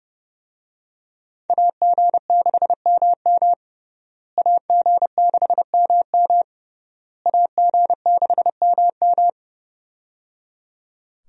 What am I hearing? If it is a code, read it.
Morse code, heard as a keyed tone, about 30 wpm, 700 Hz. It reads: AG6MM AG6MM AG6MM
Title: USA Call Signs - 5 Characters - Repeated 3x - 30wpm